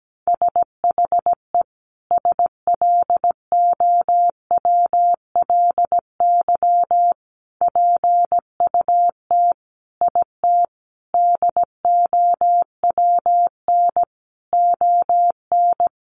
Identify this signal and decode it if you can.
Morse code, heard as a keyed tone, about 17 wpm, 700 Hz. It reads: SHE SLOWLY PUT IT DOWN ON
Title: Sentences from Top 1100 Words 17wpm